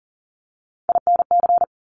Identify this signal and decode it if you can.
Morse code, heard as a keyed tone, about 40 wpm, 690 Hz. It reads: INC
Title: Top Three Letter Combinations in Words 40wpm